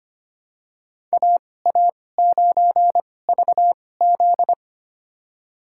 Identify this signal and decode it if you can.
Morse code, heard as a keyed tone, about 25 wpm, spaced as for 18 wpm, 700 Hz. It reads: AA9VZ
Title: USA Call Signs - 5 Characters - 2x Character Spacing - 25wpm